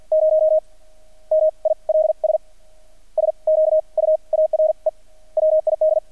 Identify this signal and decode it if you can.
Morse code, heard as a keyed tone, about 50 wpm, spaced as for 38 wpm, 630 Hz. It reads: 0 MILS SQUARE WID
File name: archived / CRSnet